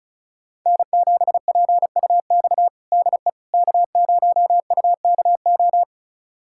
Morse code, heard as a keyed tone, about 34 wpm, 700 Hz. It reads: N7PUX DE K0UKO
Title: USA Callsign DE Callsign - 5 Characters - 35wpm